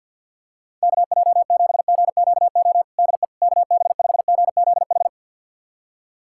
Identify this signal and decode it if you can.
Morse code, heard as a keyed tone, about 51 wpm, 700 Hz. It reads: KJ7GQO DE KB5GZH
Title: USA Callsign DE Callsign - 6 Characters - 50wpm